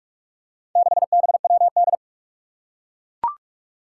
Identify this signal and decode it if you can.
Morse code, heard as a keyed tone, about 42 wpm, 700 Hz. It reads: 6BWD
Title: Four Challenging Letter-Number Combinations 45wpm